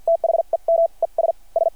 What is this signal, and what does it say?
Morse code, heard as a keyed tone, about 52 wpm, spaced as for 41 wpm, 660 Hz. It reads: THEM ES S
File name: archived / CRSnet